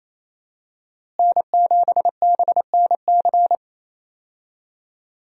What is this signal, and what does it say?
Morse code, heard as a keyed tone, about 28 wpm, 700 Hz. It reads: N7BNC
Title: USA Call Signs - 5 Characters - Repeated 2x - 28wpm